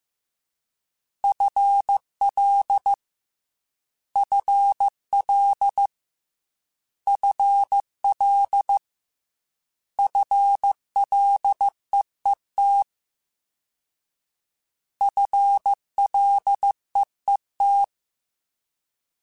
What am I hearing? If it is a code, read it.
Morse code, heard as a keyed tone, about 15 wpm, 800 Hz. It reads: FL FL FL FLEET FLEET
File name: FL / FL mp3s / FL Prefixes 3x2 Twice @13